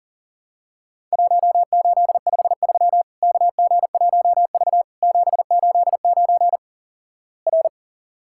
Morse code, heard as a keyed tone, about 37 wpm, 700 Hz. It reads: 1853 KG1V 789 R
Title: Worked All Europe DX Contest - QTC Exchange - 40wpm